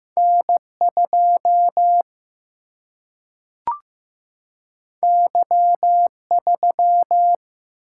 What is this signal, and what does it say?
Morse code, heard as a keyed tone, about 15 wpm, 700 Hz. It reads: N2 Y3
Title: Single-Letter Single-Number 15wpm